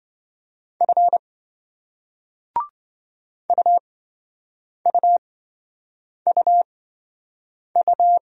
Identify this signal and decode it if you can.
Morse code, heard as a keyed tone, about 28 wpm, spaced as for 7 wpm, 700 Hz. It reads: F UUUU